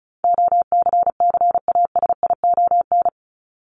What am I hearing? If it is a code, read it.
Morse code, heard as a keyed tone, about 34 wpm, 700 Hz. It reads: OCCASION
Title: Top 1501-1750 Words 35wpm